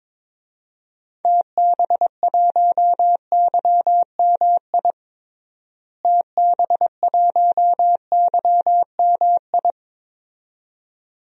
Morse code, heard as a keyed tone, about 22 wpm, 700 Hz. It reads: TB1YMI TB1YMI
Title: DXCC-100 Call Signs - 6 Characters - Repeated 2x - 22wpm